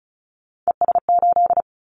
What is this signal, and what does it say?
Morse code, heard as a keyed tone, about 36 wpm, 700 Hz. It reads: ES8
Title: Callsign Prefixes for Top 100 DXCC - 35wpm